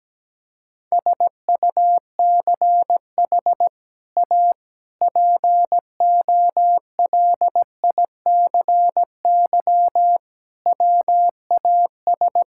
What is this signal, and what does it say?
Morse code, heard as a keyed tone, about 17 wpm, 700 Hz. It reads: SUCH A POLICY WAS
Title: Sentences from Top 1000 Words - Review Part 5 - 17wpm